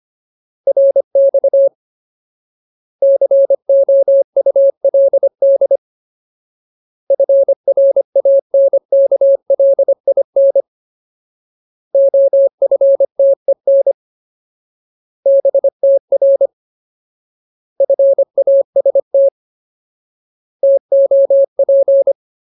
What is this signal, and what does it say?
Morse code, heard as a keyed tone, about 25 wpm, 550 Hz. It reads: RX COULD FRANKLIN OFTEN BTR FAST TOP
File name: Ultimate Word List / 25WPM Ultimate